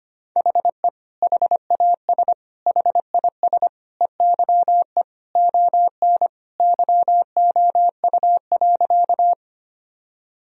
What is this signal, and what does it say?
Morse code, heard as a keyed tone, about 25 wpm, 700 Hz. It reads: HE HAS HIS EYE ON YOU.
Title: Sentences from Top 1000 Words - Review Part 1 - 25wpm